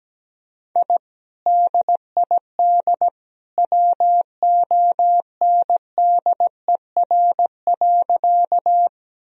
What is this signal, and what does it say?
Morse code, heard as a keyed tone, about 17 wpm, 700 Hz. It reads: I DID WONDER.
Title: Sentences from Top 1000 Words - Review Part 3 - 17wpm